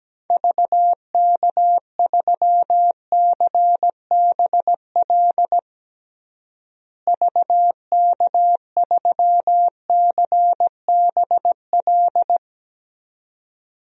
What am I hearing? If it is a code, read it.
Morse code, heard as a keyed tone, about 17 wpm, 700 Hz. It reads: VK3CBL VK3CBL
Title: DXCC-100 Call Signs - 6 Characters - Repeated 2x - 17wpm